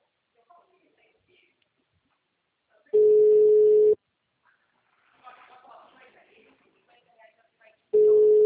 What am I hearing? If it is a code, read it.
Ringback tone (Europe, 425 Hz), with speech.